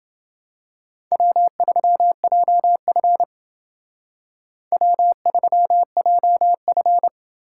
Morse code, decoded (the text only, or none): W3JF W3JF